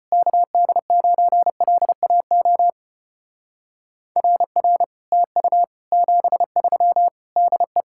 Morse code, heard as a keyed tone, about 33 wpm, 700 Hz. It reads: KD9LAO RR TU 73 DE